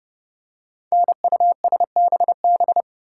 Morse code, heard as a keyed tone, about 28 wpm, 700 Hz. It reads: NUSBB
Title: Code Groups - 5 Letters - 30wpm